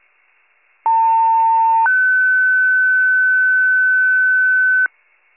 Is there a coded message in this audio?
Yes: two-tone page from 0.8 to 4.9 s, heard as a one-second tone, then a three-second tone, about 900 then 1500 Hz, with noise.